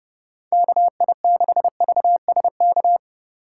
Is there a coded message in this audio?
Yes: Morse code from 0.5 to 3.0 s, heard as a keyed tone, about 29 wpm, 700 Hz. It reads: KI6VSK